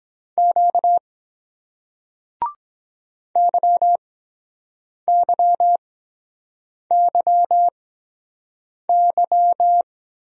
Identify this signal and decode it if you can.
Morse code, heard as a keyed tone, about 23 wpm, spaced as for 7 wpm, 700 Hz. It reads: Q YYYY